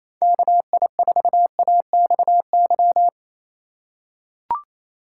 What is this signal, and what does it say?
Morse code, heard as a keyed tone, about 28 wpm, 700 Hz. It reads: KI4AXY
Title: USA Call Signs - 6 Characters - 3x Character Spacing - 28wpm